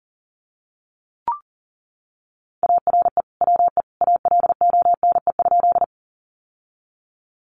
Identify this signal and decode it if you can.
Morse code, heard as a keyed tone, about 40 wpm, 700 Hz. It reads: ARE WE ALONE?
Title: Sentences from Top 1000 Words 40wpm